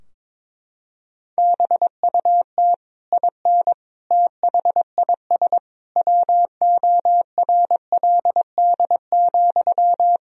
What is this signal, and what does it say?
Morse code, heard as a keyed tone, about 21 wpm, 700 Hz. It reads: BUT IN THIS WORLD,
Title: Sentences from Top 1000 Words - Review Part 4 - 22wpm